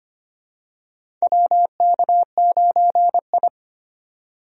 Morse code, heard as a keyed tone, about 25 wpm, 700 Hz. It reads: WK9I
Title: USA Call Signs - 4 Characters - 25wpm